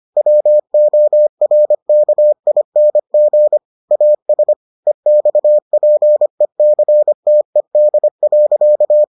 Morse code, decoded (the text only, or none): WORKING AS EXPECTED.